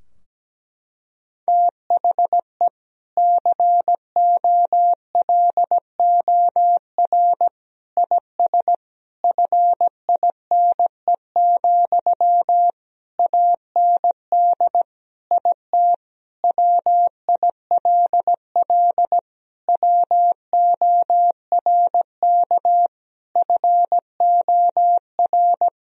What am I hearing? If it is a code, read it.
Morse code, heard as a keyed tone, about 17 wpm, 700 Hz. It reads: THE COLOR IS FINE, AND IT WILL WORK FOR
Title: Sentences from Top 1000 Words - Review Part 2 - 17wpm